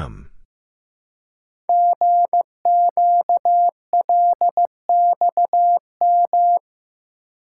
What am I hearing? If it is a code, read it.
Morse code, heard as a keyed tone, about 15 wpm, 700 Hz. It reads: GQLXM
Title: Code Groups - 5 Letters - 15wpm